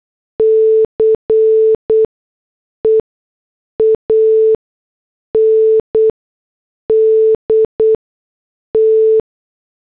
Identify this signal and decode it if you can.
Morse code, heard as a keyed tone, about 8 wpm, spaced as for 6 wpm, 440 Hz. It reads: CE AND T